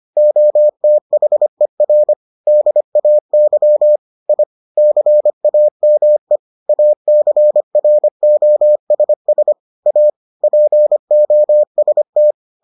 Morse code, heard as a keyed tone, about 25 wpm, 600 Hz. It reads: OTHER DAY I CAME ACROSS A POST